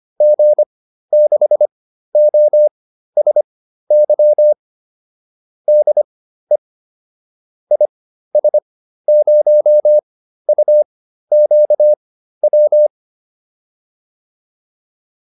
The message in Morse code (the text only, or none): G6OSY DE IS0UQW